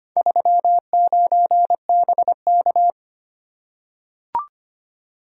Morse code, heard as a keyed tone, about 25 wpm, 700 Hz. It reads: 39BK
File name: practice / 25wpm / Four Letter-Number Combinations - 25wpm